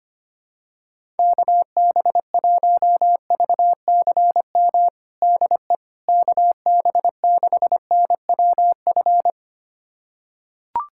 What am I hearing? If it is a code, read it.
Morse code, heard as a keyed tone, about 25 wpm, 700 Hz. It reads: KB1VCM DE KB6NWF
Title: USA Callsign DE Callsign - 6 Characters - 25wpm